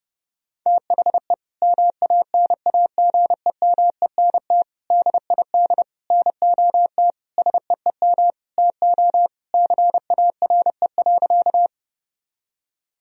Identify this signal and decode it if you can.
Morse code, heard as a keyed tone, about 30 wpm, 700 Hz. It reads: THE MANAGEMENT DID NOT SEEM TO CARE.
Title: Sentences from Top 1000 Words - Review Part 4 - 30wpm